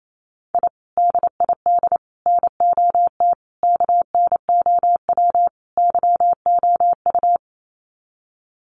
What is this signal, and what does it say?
Morse code, heard as a keyed tone, about 28 wpm, 700 Hz. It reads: I DID NOT KNOW YOU
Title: Sets of 5 Words Encore2 28wpm